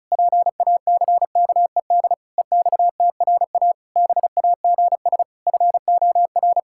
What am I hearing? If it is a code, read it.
Morse code, heard as a keyed tone, about 34 wpm, 700 Hz. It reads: PACKED EXTRA BAGS FOR
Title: Pangram Challenge v2 - 35wpm